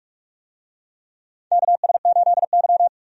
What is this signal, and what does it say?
Morse code, heard as a keyed tone, about 43 wpm, 700 Hz. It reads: KS8Y
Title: USA Call Signs - 4 Characters - 45wpm